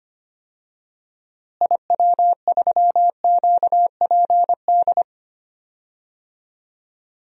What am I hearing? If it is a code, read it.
Morse code, heard as a keyed tone, about 25 wpm, 700 Hz. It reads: IW3QPD
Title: DXCC-100 Call Signs - 6 Characters - Repeated 3x - 25wpm